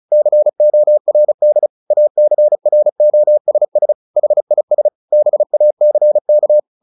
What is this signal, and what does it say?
Morse code, heard as a keyed tone, about 34 wpm, 600 Hz. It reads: CORD ACROSS HIS BACK
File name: the old man and the sea / the old man and the sea 1 35wpm